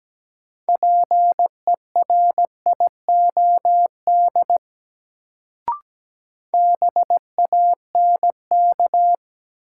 Morse code, heard as a keyed tone, about 17 wpm, 700 Hz. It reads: PERIOD BANK
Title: Sets of 5 Words - New Word Drill - 17wpm